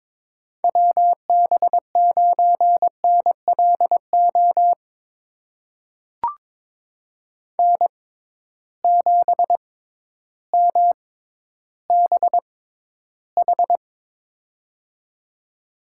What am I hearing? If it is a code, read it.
Morse code, heard as a keyed tone, about 22 wpm, spaced as for 12 wpm, 700 Hz. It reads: WB9NLO N7MBH
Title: USA Call Signs - 6x Character Spacing - 22wpm